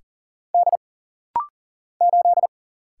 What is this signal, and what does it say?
Morse code, heard as a keyed tone, about 40 wpm, 700 Hz. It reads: D 8
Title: Single Letter-Number - Rapid-Fire 40wpm